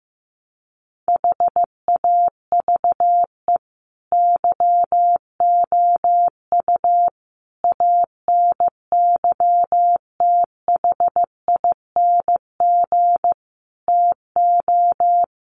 Morse code, heard as a keyed tone, about 15 wpm, 700 Hz. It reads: HAVE YOU ANYTHING TO
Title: Sentences from Top 1100 Words 15wpm